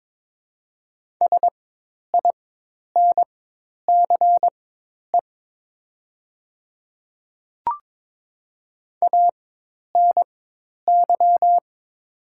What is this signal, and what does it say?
Morse code, heard as a keyed tone, about 22 wpm, spaced as for 10 wpm, 700 Hz. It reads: SINCE ANY
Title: 4x Character Spacing - Top 300 Words - 22wpm